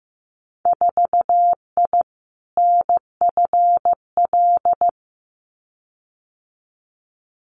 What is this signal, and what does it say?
Morse code, heard as a keyed tone, about 15 wpm, 700 Hz. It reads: 4I NFL
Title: Winter Field Day Exchanges 15wpm